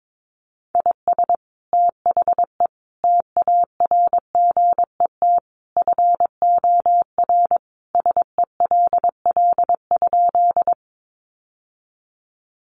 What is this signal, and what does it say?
Morse code, heard as a keyed tone, about 22 wpm, 700 Hz. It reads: IS THE TARGET FOR SELL?
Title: Sentences from Top 1200 Words 22wpm